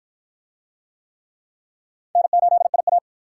Morse code, heard as a keyed tone, about 53 wpm, 700 Hz. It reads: N8IA